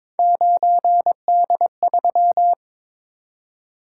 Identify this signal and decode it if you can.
Morse code, heard as a keyed tone, about 22 wpm, 700 Hz. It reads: 9D3
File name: practice / 22wpm / Three Letter-Number Combinations - 22wpm